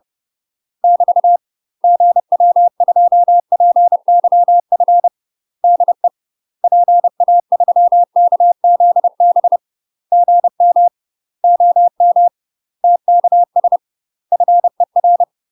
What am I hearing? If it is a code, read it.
Morse code, heard as a keyed tone, about 30 wpm, 700 Hz. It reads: <BT> GW2PYF DE PA3KZB GM OM TKS FER
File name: UK MASTER QSOs TRAINING / 30WPM / UK MASTER QSOs 0026